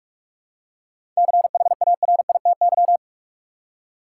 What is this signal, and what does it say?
Morse code, heard as a keyed tone, about 42 wpm, 700 Hz. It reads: CHARITY
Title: Seven Letter Words - 45wpm